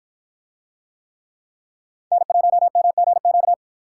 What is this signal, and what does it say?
Morse code, heard as a keyed tone, about 53 wpm, 700 Hz. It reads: N1MGQ